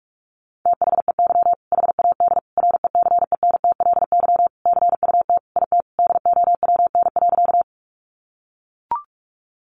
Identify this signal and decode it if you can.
Morse code, heard as a keyed tone, about 44 wpm, 700 Hz. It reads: THEY HAD RECENTLY CUT IT DOWN.
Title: Sentences from Top 1200 Words 45wpm